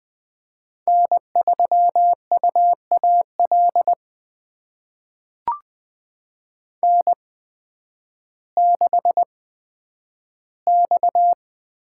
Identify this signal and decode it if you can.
Morse code, heard as a keyed tone, about 20 wpm, spaced as for 11 wpm, 700 Hz. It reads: N3UAL N6X
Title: USA Call Signs - 5 Characters - 8x Character Spacing - 20wpm